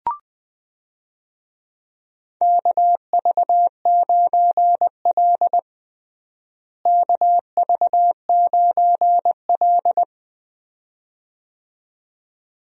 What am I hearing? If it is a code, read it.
Morse code, heard as a keyed tone, about 20 wpm, 700 Hz. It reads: KV9L KV9L
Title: USA Call Signs - 4 Characters - Repeated 2x - 20wpm